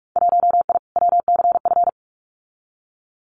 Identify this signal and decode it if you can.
Morse code, heard as a keyed tone, about 46 wpm, 690 Hz. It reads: 1I WCF
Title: Winter Field Day Exchanges Encore 45wpm